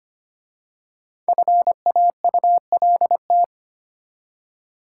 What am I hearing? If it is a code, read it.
Morse code, heard as a keyed tone, about 25 wpm, 700 Hz. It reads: FAULT
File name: 5 Letter Words - First 3 Letters Given - 25wpm